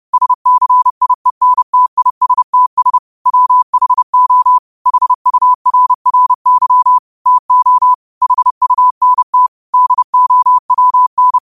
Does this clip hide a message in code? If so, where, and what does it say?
Yes: Morse code from 0.1 to 11.4 s, heard as a keyed tone, about 30 wpm, 1000 Hz. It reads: SCIENTISTS WHO HURRY TO HUNT DOWN